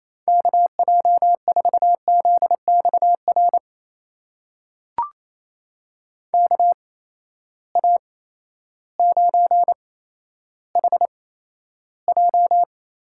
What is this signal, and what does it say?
Morse code, heard as a keyed tone, about 28 wpm, spaced as for 13 wpm, 700 Hz. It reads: KJ4ZXR KA9HJ
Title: USA Call Signs - 6 Characters - 8x Character Spacing - 28wpm